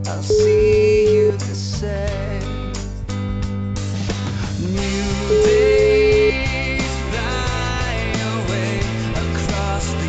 Ringback tone (Europe, 425 Hz), with noise.